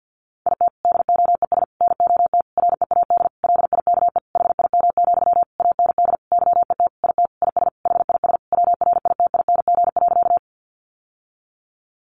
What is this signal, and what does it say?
Morse code, heard as a keyed tone, about 50 wpm, 700 Hz. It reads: IT DOES NOT READ LIKE HIM, AND YET IT IS HIS WRITING.
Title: Sentences from Top 800 Words 50wpm